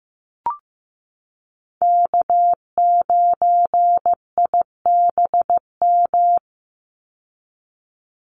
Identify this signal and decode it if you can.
Morse code, heard as a keyed tone, about 15 wpm, 700 Hz. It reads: K9IBM
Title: Call Signs Repeated 2x - 15wpm